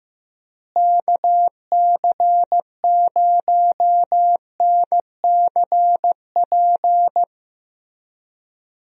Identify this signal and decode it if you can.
Morse code, heard as a keyed tone, about 15 wpm, 700 Hz. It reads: KC0NCP
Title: USA Call Signs - 6 Characters - Repeated 3x - 15wpm